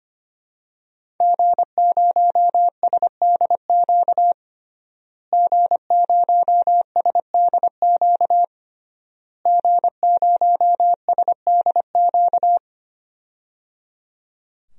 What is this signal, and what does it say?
Morse code, heard as a keyed tone, about 25 wpm, 700 Hz. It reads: G0SDQ G0SDQ G0SDQ
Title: DXCC-100 Call Signs - 5 Characters - Repeated 3x - 25wpm